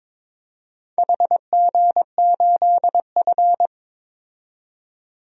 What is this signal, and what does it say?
Morse code, heard as a keyed tone, about 22 wpm, 700 Hz. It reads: HG8F
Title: DXCC-100 Call Signs - 4 Characters - 22wpm